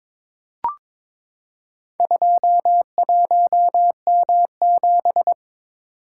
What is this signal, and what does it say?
Morse code, heard as a keyed tone, about 22 wpm, 700 Hz. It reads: 21M7
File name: practice / 22wpm / Four Letter-Number Combinations - 22wpm